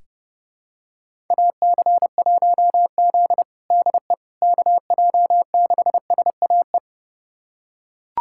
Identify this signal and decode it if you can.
Morse code, heard as a keyed tone, about 30 wpm, 700 Hz. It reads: AC1Z DE KJ6SAE
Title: USA Callsign DE Callsign 30wpm